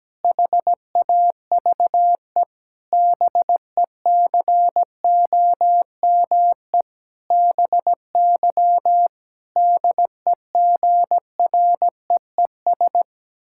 Morse code, heard as a keyed tone, about 17 wpm, 700 Hz. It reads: HAVE BECOME BY DEGREES
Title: Pangram Challenge v2 - 17wpm